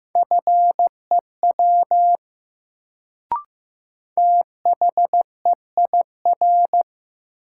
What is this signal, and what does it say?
Morse code, heard as a keyed tone, about 15 wpm, 700 Hz. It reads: FEW THEIR
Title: Sets of 2 Words - New Word Drill - 15wpm